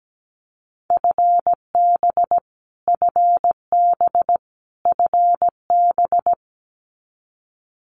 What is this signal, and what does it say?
Morse code, heard as a keyed tone, about 17 wpm, 700 Hz. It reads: FB FB FB
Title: Abbreviations Repeated 3x - 17wpm